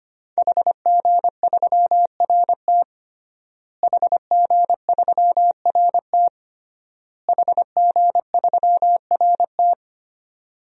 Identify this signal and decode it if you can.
Morse code, heard as a keyed tone, about 25 wpm, 700 Hz. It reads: HG3RT HG3RT HG3RT
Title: DXCC-100 Call Signs - 5 Characters - Repeated 3x - 25wpm